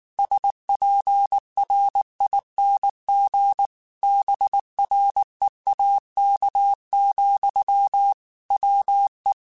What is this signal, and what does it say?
Morse code, heard as a keyed tone, about 19 wpm, 800 Hz. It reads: SPRING BREAK, WE
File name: QSOs / QSOs-mp3s / WA7QOX 1@18